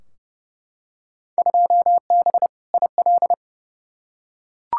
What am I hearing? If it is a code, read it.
Morse code, heard as a keyed tone, about 32 wpm, 700 Hz. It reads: 2B IL
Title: Field Day Exchanges Encore - 30wpm